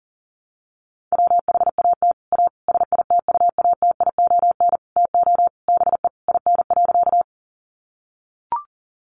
Morse code, heard as a keyed tone, about 40 wpm, 700 Hz. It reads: WHAT A SITUATION TO BE IN.
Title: Sentences from Top 1200 Words 40wpm